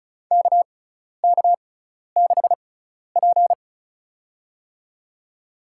Morse code, heard as a keyed tone, about 36 wpm, spaced as for 12 wpm, 700 Hz. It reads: KK6P
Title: USA Call Signs - 4 Characters - 6x Character Spacing - 35wpm